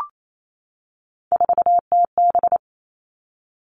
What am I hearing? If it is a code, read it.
Morse code, heard as a keyed tone, about 28 wpm, 690 Hz. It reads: 4TB
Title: Three Challenging Letter-Number Combinations 28wpm